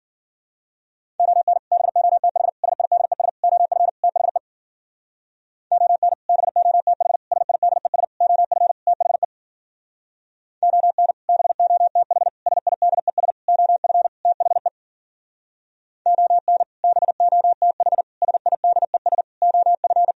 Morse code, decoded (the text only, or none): ON BOTH SIDES OF THE ON BOTH SIDES OF THE ON BOTH SIDES OF THE ON BOTH SIDES OF